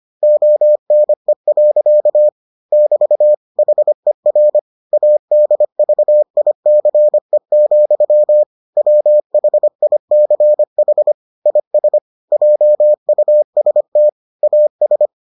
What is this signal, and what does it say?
Morse code, heard as a keyed tone, about 25 wpm, 600 Hz. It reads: ONE. <BT> HER ADVICE, WHICH IS JUST AS